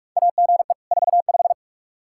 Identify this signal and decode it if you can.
Morse code, heard as a keyed tone, about 42 wpm, 700 Hz. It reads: AGE 45
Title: Practice QSO - Exchange 3 - More Info - 45wpm